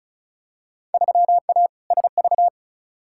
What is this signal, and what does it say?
Morse code, heard as a keyed tone, about 34 wpm, 700 Hz. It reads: 3A SV
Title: Field Day Exchanges - 35wpm